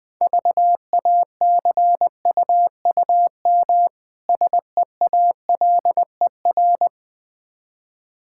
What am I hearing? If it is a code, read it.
Morse code, heard as a keyed tone, about 20 wpm, 690 Hz. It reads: VACUUM SEALER
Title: Difficult Words - Speed-Racing with Hints - 20wpm